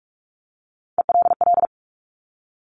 Morse code, heard as a keyed tone, about 46 wpm, 700 Hz. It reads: ELL
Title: Top Three Letter Combinations in Words 45wpm